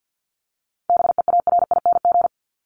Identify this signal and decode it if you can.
Morse code, heard as a keyed tone, about 50 wpm, 700 Hz. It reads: BEARING
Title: ING Suffix - Top 100 Words with Base Words 50wpm